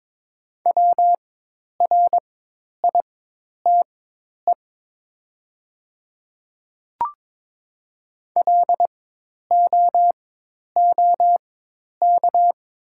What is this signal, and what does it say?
Morse code, heard as a keyed tone, about 22 wpm, spaced as for 10 wpm, 700 Hz. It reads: WRITE LOOK
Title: 4x Character Spacing - Top 100 Words - 22wpm